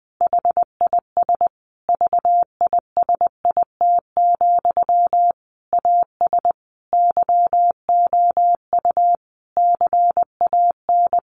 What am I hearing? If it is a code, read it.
Morse code, heard as a keyed tone, about 20 wpm, 700 Hz. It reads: HIS VISIT, AS YOU CAN
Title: Sentences from Top 1400 Words 20wpm